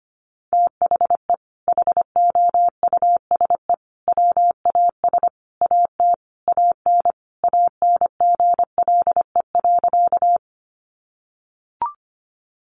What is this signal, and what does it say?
Morse code, heard as a keyed tone, about 25 wpm, 700 Hz. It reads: THE HOUSE WAS AT AN ANGLE.
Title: Sentences from Top 1500 Words 25wpm